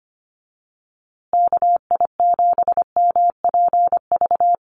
Morse code, heard as a keyed tone, about 25 wpm, 700 Hz. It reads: KI7MPV
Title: USA Call Signs Repeated 3x 25wpm